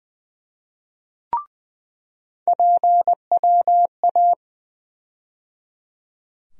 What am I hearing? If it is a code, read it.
Morse code, heard as a keyed tone, about 20 wpm, 700 Hz. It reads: PWA
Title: USA Call Sign Postfixes - 3 Letters 20wpm